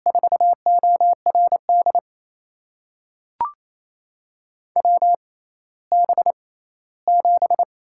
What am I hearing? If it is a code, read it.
Morse code, heard as a keyed tone, about 28 wpm, spaced as for 16 wpm, 700 Hz. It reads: 4ORD WB7
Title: USA Call Signs - 6 Characters - 6x Character Spacing - 28wpm